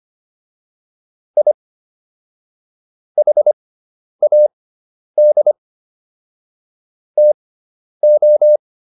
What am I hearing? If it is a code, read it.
Morse code, heard as a keyed tone, about 25 wpm, spaced as for 10 wpm, 600 Hz. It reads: I HAD TO